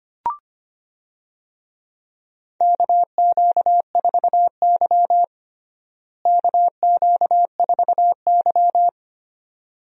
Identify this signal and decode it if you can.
Morse code, heard as a keyed tone, about 25 wpm, 700 Hz. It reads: KQ4Y KQ4Y
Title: USA Call Signs - 4 Characters - Repeated 2x - 25wpm